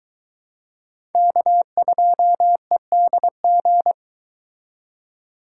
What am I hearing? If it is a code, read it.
Morse code, heard as a keyed tone, about 23 wpm, 700 Hz. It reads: K2EDG